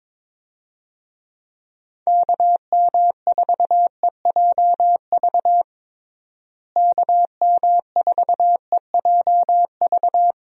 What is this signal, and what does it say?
Morse code, heard as a keyed tone, about 22 wpm, 700 Hz. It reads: KM4EJV KM4EJV
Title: USA Call Signs - 6 Characters - Repeated 2x - 22wpm